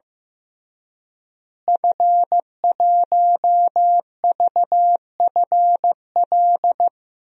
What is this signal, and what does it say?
Morse code, heard as a keyed tone, about 15 wpm, 700 Hz. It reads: F1VFL
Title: DXCC-100 Call Signs - 5 Characters - Repeated 3x - 15wpm